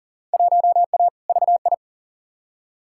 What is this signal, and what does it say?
Morse code, heard as a keyed tone, about 37 wpm, 700 Hz. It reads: 1A VI
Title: Field Day Exchanges Encore - 40wpm